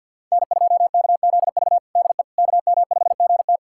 Morse code, heard as a keyed tone, about 50 wpm, 700 Hz. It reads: N2KZV DE KM5GT
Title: USA Callsign DE Callsign - 5 Characters - 50wpm